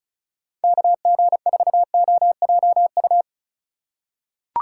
Morse code, heard as a keyed tone, about 34 wpm, 700 Hz. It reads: KG4OJU
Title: USA Call Signs - 6 Characters - 4x Character Spacing - 35wpm